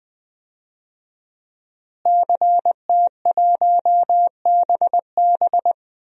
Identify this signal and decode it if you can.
Morse code, heard as a keyed tone, about 20 wpm, 700 Hz. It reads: CT1BB
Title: DXCC-100 Call Signs - 5 Characters - Repeated 2x - 20wpm